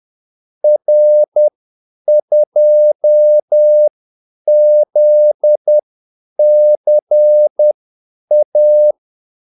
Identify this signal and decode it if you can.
Morse code, heard as a keyed tone, about 10 wpm, spaced as for 8 wpm, 600 Hz. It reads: R2ZCA
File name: Callsigns-10-8wpm